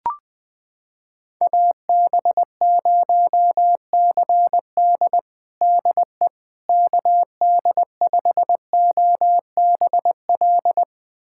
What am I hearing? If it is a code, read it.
Morse code, heard as a keyed tone, about 20 wpm, 700 Hz. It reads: AB0CD DE KD5OBL
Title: USA Callsign DE Callsign 20wpm